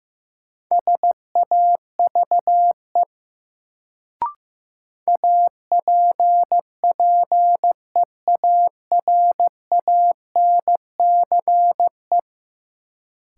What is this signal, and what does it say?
Morse code, heard as a keyed tone, about 15 wpm, 700 Hz. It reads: SAVE APPEARANCE
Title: Sentences from Top 1100 Words - New Word Drill - 15wpm